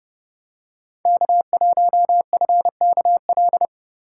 Morse code, heard as a keyed tone, about 29 wpm, 700 Hz. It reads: K1FKL